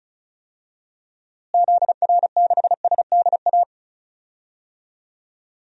Morse code, heard as a keyed tone, about 34 wpm, 700 Hz. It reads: ZR6SDA
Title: DXCC-100 Call Signs - 6 Characters - Repeated 3x - 35wpm